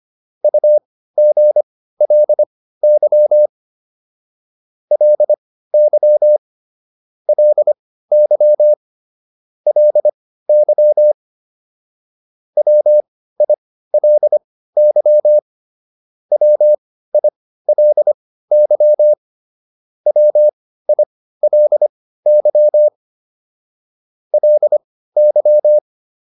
Morse code, heard as a keyed tone, about 25 wpm, spaced as for 15 wpm, 600 Hz. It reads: UGLY LY LY LY WILY WILY WILY LY
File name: LY4 15